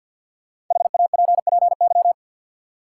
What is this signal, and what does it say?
Morse code, heard as a keyed tone, about 50 wpm, 700 Hz. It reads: HAPPY
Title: Sentences from Top 1000 Words - New Word Drill - 50wpm